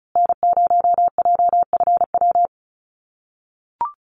Morse code, heard as a keyed tone, about 34 wpm, 700 Hz. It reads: N0JFW
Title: USA Call Signs Repeated 2x 35wpm